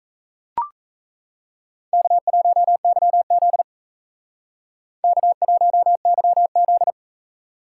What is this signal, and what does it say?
Morse code, heard as a keyed tone, about 36 wpm, 700 Hz. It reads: K1YZ K1YZ